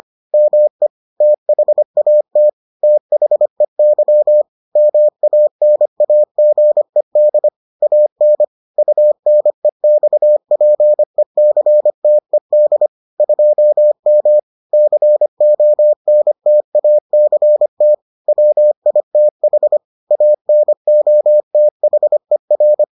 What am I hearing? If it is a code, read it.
Morse code, heard as a keyed tone, about 25 wpm, 600 Hz. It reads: ME THAT THEY MANAGED AN UNEXPECTED 2M CONTACT WITH ANOTHER